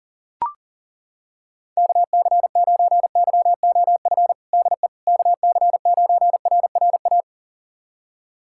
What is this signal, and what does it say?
Morse code, heard as a keyed tone, about 40 wpm, 700 Hz. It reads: KC9YOF DE KC9RRA
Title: USA Callsign DE Callsign 40wpm